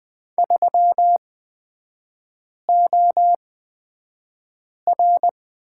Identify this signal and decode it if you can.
Morse code, heard as a keyed tone, about 20 wpm, spaced as for 5 wpm, 700 Hz. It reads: 3OR
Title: 236 Lesson 38e - Letter q - Three Characters 20wpm